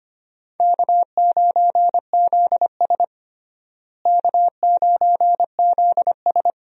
Morse code, heard as a keyed tone, about 25 wpm, 700 Hz. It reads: K9ZS K9ZS
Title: USA Call Signs - 4 Characters - Repeated 3x - 25wpm